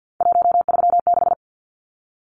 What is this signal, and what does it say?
Morse code, heard as a keyed tone, about 50 wpm, 700 Hz. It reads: 136
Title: Numbers - 3-Digit 50wpm